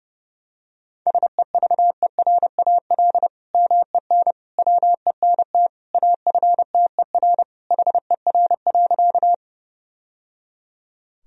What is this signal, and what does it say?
Morse code, heard as a keyed tone, about 32 wpm, 700 Hz. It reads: SEVERAL MEN WENT AFTER HER.
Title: Sentences from Top 1000 Words - Review Part 2 - 30wpm